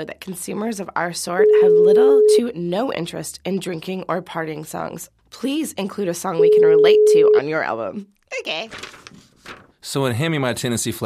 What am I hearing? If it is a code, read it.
Ringback tone (Europe, 425 Hz), with speech.